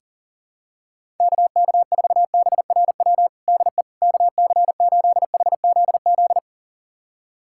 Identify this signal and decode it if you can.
Morse code, heard as a keyed tone, about 40 wpm, 700 Hz. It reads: KK4BRW DE KC8HZZ
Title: USA Callsign DE Callsign - 6 Characters - 40wpm